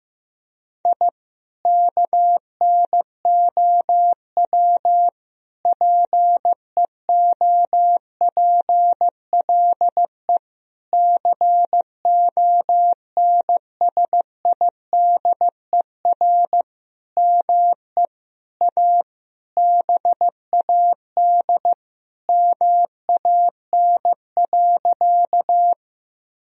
Morse code, heard as a keyed tone, about 15 wpm, 700 Hz. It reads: I KNOW PEOPLE CONSIDER ME A BAD MAN.
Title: Sentences from Top 1000 Words - Review Part 4 - 15wpm